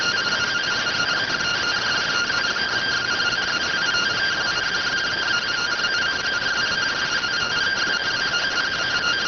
Two-tone teleprinter signal, about 170 Hz shift.